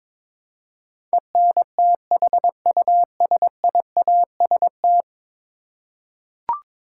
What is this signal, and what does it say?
Morse code, heard as a keyed tone, about 22 wpm, 700 Hz. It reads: ENTHUSIAST
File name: practice / 22wpm / 10 Letter Words - First 3 Letters Given - 22wpm